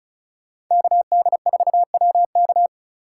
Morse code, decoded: KD4WK

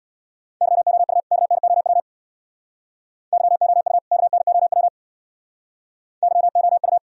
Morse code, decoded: YOU KNOW YOU KNOW YOU

